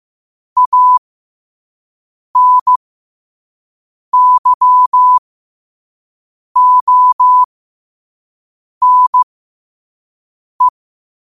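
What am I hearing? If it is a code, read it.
Morse code, heard as a keyed tone, about 15 wpm, spaced as for 5 wpm, 1000 Hz. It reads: ANYONE
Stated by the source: Morse code sent at 5 words per minute.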